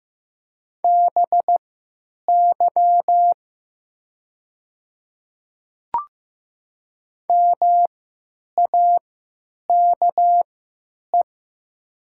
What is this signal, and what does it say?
Morse code, heard as a keyed tone, about 15 wpm, spaced as for 9 wpm, 690 Hz. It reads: BY MAKE
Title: 3x Character Spacing - Top 100 Words - 15wpm